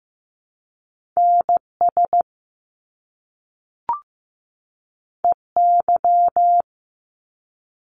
Morse code, heard as a keyed tone, about 15 wpm, 700 Hz. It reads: NS EY
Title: Top Two Letter Combinations in Words 15wpm